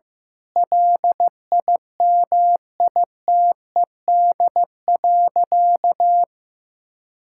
Morse code, heard as a keyed tone, about 15 wpm, 700 Hz. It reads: LIMITED.
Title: Sentences from Top 1000 Words - Review Part 6 - 15wpm